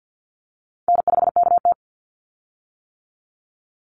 Morse code, heard as a keyed tone, about 50 wpm, 700 Hz. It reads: N5KT
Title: Easy Call Signs - 50wpm